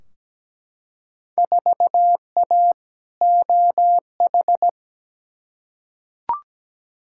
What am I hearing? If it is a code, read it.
Morse code, heard as a keyed tone, about 17 wpm, 700 Hz. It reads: 4A OH
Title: Field Day Exchanges - 17wpm